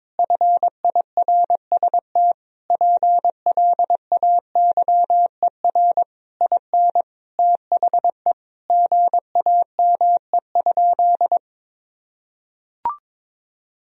Morse code, heard as a keyed tone, about 22 wpm, 700 Hz. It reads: FIRST PLAYER IN THE GAME?
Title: Sentences from Top 1000 Words - Review Part 4 - 22wpm